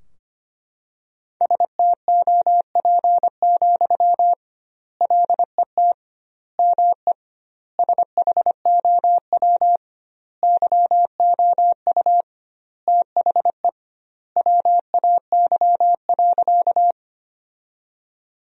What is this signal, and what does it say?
Morse code, heard as a keyed tone, about 25 wpm, 700 Hz. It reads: STOP, LET ME SHOW YOU THE WAY.
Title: 2x Word-Spacing - Sentences from Top 300 Words - 25wpm